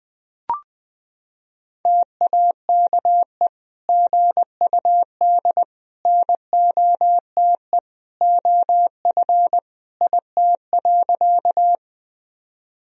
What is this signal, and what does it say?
Morse code, heard as a keyed tone, about 20 wpm, 700 Hz. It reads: TAKE GUD NOTE OF IT.
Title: Sentences from Top 500 Words and Abbreviations 20wpm